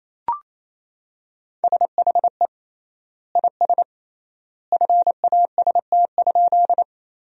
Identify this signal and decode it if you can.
Morse code, heard as a keyed tone, about 28 wpm, 700 Hz. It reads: SHE IS FAST?
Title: 3x Word-Spacing - Sentences from Top 400 Words - 28wpm